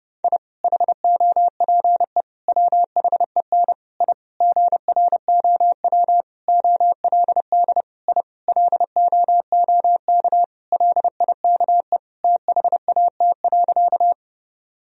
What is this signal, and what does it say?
Morse code, heard as a keyed tone, about 30 wpm, 700 Hz. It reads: I HOPE WHEN I GROW OLD I LOOK LIKE THAT.
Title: Sentences from Top 1000 Words - Review Part 1 - 30wpm